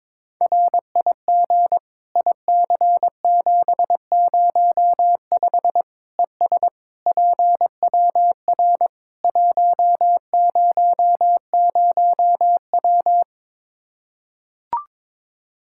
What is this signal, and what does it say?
Morse code, heard as a keyed tone, about 22 wpm, 700 Hz. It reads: RIG IC705 ES PWR 100W
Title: Practice QSO - Exchange 2 - Info - 22wpm